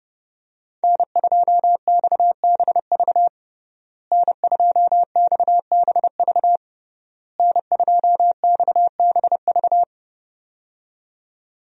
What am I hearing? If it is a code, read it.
Morse code, heard as a keyed tone, about 30 wpm, 700 Hz. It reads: N2XBV N2XBV N2XBV
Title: USA Call Signs - 5 Characters - Repeated 3x - 30wpm